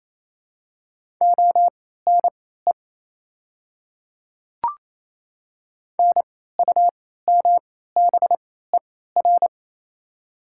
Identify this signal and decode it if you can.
Morse code, heard as a keyed tone, about 28 wpm, spaced as for 16 wpm, 700 Hz. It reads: ONE NUMBER
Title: 3x Character Spacing - Top 300 Words - 28wpm